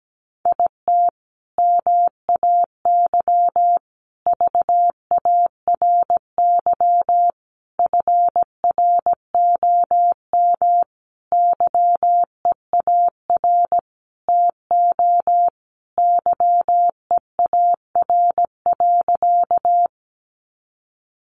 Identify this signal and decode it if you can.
Morse code, heard as a keyed tone, about 17 wpm, 700 Hz. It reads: IT MAY VARY FROM YEAR TO YEAR.
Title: Sentences from Top 1200 Words 17wpm